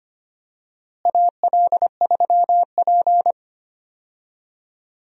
Morse code, heard as a keyed tone, about 25 wpm, 700 Hz. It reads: AL3P
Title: USA Call Signs - 4 Characters - Repeated 3x - 25wpm